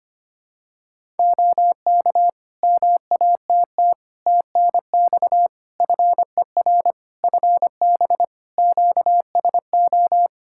Morse code, heard as a keyed tone, about 25 wpm, 700 Hz. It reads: OK MATT TNX FER FB QSO
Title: Practice QSO - Exchange 4 - Ending - 25wpm